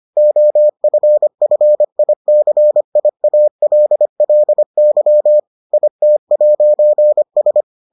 Morse code, heard as a keyed tone, about 25 wpm, 600 Hz. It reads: OFFICIALLY IT'S